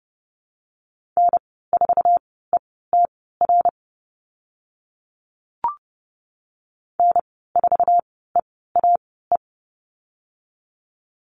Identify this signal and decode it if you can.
Morse code, heard as a keyed tone, about 30 wpm, spaced as for 17 wpm, 700 Hz. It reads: N4ETR N4EAE
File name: Lesson 15g - Number 5 - Call Signs 30wpm